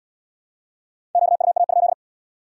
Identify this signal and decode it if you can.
Morse code, heard as a keyed tone, about 73 wpm, 700 Hz. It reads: QUA?